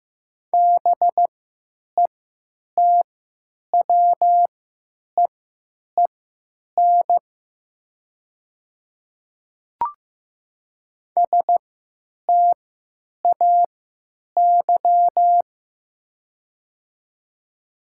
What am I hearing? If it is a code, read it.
Morse code, heard as a keyed tone, about 15 wpm, spaced as for 9 wpm, 700 Hz. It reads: BETWEEN STAY
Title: 3x Character Spacing - Top 500 Words - 15wpm